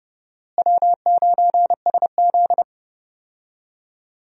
Morse code, heard as a keyed tone, about 30 wpm, 700 Hz. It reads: W9SZ
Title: USA Call Signs - 4 Characters - Repeated 3x - 30wpm